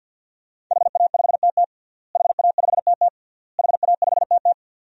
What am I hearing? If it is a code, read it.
Morse code, heard as a keyed tone, about 52 wpm, 700 Hz. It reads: HA5TT HA5TT HA5TT
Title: DXCC-100 Call Signs - 5 Characters - Repeated 3x - 50wpm